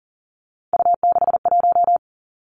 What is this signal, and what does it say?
Morse code, heard as a keyed tone, about 40 wpm, 700 Hz. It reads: U61
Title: Three Challenging Letter-Number Combinations 40wpm